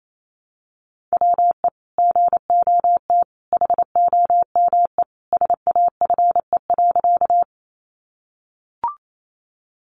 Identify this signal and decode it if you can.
Morse code, heard as a keyed tone, about 28 wpm, 700 Hz. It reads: WE GOT HOME SAFE.
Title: Sentences from Top 900 Words 28wpm